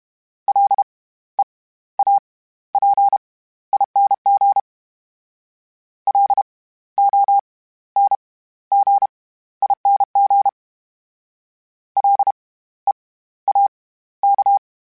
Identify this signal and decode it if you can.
Morse code, heard as a keyed tone, about 32 wpm, spaced as for 15 wpm, 800 Hz. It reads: LEAPING LONGING LEAK
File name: Special INGs / Special INGs-mp3s / SlowToFaster ing @30